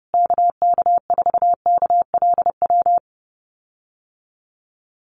Morse code, heard as a keyed tone, about 30 wpm, 700 Hz. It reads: KK4KLW
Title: USA Call Signs Repeated 3x 30wpm